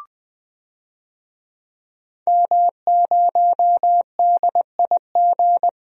Morse code, heard as a keyed tone, about 20 wpm, 700 Hz. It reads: M0DIG
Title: DXCC-100 Call Signs - 5 Characters - 20wpm